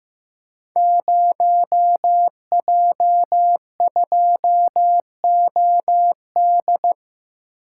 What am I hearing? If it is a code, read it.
Morse code, heard as a keyed tone, about 15 wpm, 700 Hz. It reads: 0J2OD
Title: Code Groups - 5 Letter:Numbers - 15wpm